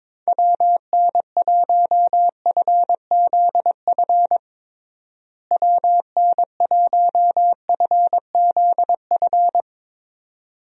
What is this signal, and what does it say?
Morse code, heard as a keyed tone, about 22 wpm, 700 Hz. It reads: WN1FZF WN1FZF
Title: USA Call Signs - 6 Characters - Repeated 2x - 22wpm